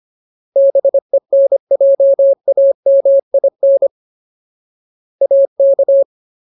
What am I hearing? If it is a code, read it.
Morse code, heard as a keyed tone, about 25 wpm, 550 Hz. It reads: BENJAMIN AK
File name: Ultimate Word List / 25WPM Ultimate